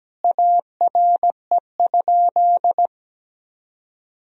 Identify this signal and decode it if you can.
Morse code, heard as a keyed tone, about 17 wpm, 700 Hz. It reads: ARE?
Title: Sentences from Top 1000 Words - Review Part 4 - 17wpm